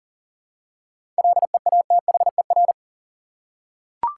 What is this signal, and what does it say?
Morse code, heard as a keyed tone, about 40 wpm, 700 Hz. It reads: LEATHER
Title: Seven Letter Words - 40wpm